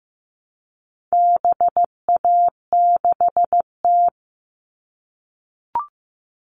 Morse code, heard as a keyed tone, about 15 wpm, 700 Hz. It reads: BA6T
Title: All Cut Numbers - 4-Digit 15wpm